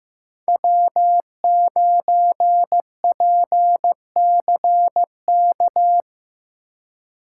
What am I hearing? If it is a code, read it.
Morse code, heard as a keyed tone, about 18 wpm, 700 Hz. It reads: W9PCK
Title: Speed Racing - USA 1x3 Call Signs 15wpm